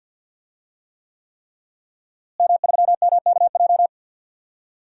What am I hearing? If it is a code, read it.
Morse code, heard as a keyed tone, about 52 wpm, 700 Hz. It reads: M3MKJ
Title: DXCC-100 Call Signs - 5 Characters - Repeated 3x - 50wpm